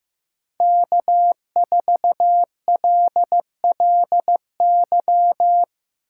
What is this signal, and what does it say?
Morse code, heard as a keyed tone, about 16 wpm, 700 Hz. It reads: K4LLY
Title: USA Call Signs - 5 Characters - 3x Character Spacing - 15wpm